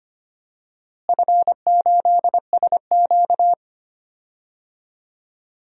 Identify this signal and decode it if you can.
Morse code, heard as a keyed tone, about 25 wpm, 700 Hz. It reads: F8SQ
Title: DXCC-100 Call Signs - 4 Characters - Repeated 2x - 25wpm